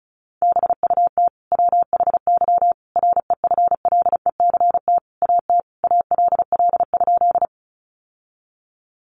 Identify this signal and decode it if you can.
Morse code, heard as a keyed tone, about 34 wpm, 700 Hz. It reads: BUT WHY REFLECT AT ALL?
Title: Sentences from Top 1500 Words 35wpm